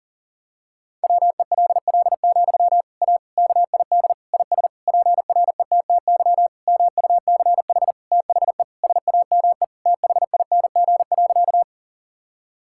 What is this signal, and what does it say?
Morse code, heard as a keyed tone, about 40 wpm, 700 Hz. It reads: WELL, A KID IS PRETTY MUCH THE SAME THING.
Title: Sentences from Top 1600 Words 40wpm